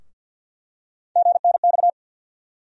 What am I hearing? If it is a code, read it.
Morse code, heard as a keyed tone, about 51 wpm, 700 Hz. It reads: GNX
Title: USA Call Sign Postfixes - 3 Letters 50wpm